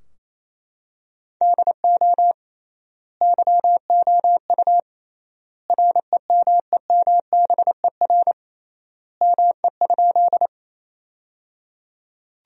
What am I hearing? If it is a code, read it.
Morse code, heard as a keyed tone, about 28 wpm, 700 Hz. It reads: DO YOU REMEMBER ME?
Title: 3x Word-Spacing - Sentences from Top 400 Words - 28wpm